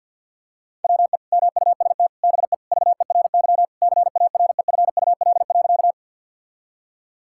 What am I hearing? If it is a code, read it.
Morse code, heard as a keyed tone, about 50 wpm, 700 Hz. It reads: WE MUST BE VERY CAREFUL.
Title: Sentences from Top 1600 Words 50wpm